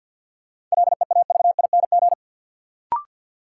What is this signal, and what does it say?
Morse code, heard as a keyed tone, about 51 wpm, 700 Hz. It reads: LEAVING
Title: Sentences from Top 1000 Words - Review Part 4 - New Word Drill - 50wpm